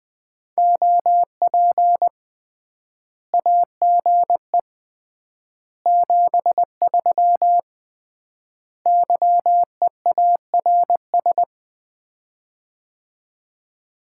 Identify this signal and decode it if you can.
Morse code, heard as a keyed tone, about 21 wpm, 700 Hz. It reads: OP AGE 73 YEARS
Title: Common QSO Phrases - 3x Word-Spacing - 20wpm